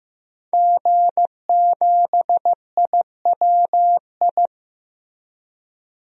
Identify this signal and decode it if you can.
Morse code, heard as a keyed tone, about 15 wpm, 700 Hz. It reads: G7IWI
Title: DXCC-100 Call Signs - 5 Characters - Repeated 3x - 15wpm